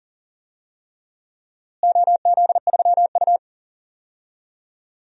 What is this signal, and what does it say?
Morse code, heard as a keyed tone, about 38 wpm, 700 Hz. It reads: OZ3U
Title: DXCC-100 Call Signs - 4 Characters - Repeated 3x - 40wpm